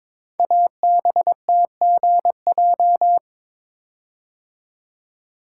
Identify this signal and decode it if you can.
Morse code, heard as a keyed tone, about 22 wpm, 700 Hz. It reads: ABTGJ
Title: Code Groups - 5 Letters - 22wpm